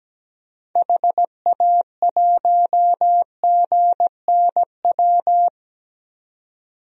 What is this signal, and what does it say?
Morse code, heard as a keyed tone, about 17 wpm, 700 Hz. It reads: HA1GNW
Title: DXCC-100 Call Signs - 6 Characters - 17wpm